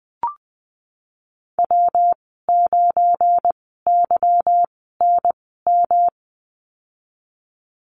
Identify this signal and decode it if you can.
Morse code, heard as a keyed tone, about 20 wpm, spaced as for 15 wpm, 700 Hz. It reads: W9YNM
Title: 179 Lesson 29g - Slash - Call Signs 20wpm